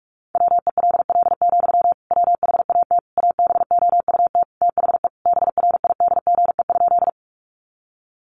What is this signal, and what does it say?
Morse code, heard as a keyed tone, about 44 wpm, 700 Hz. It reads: WELL, WHAT ABOUT THE BRIDGE?
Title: Sentences from Top 1200 Words 45wpm